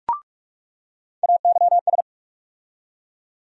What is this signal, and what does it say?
Morse code, heard as a keyed tone, about 42 wpm, 700 Hz. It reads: AYS
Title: USA Call Sign Postfixes - 3 Letters 45wpm